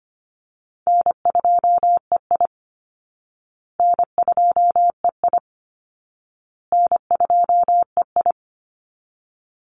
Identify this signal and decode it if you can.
Morse code, heard as a keyed tone, about 25 wpm, 700 Hz. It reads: N2EI N2EI N2EI
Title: Easy Call Signs Repeated 3x - 25wpm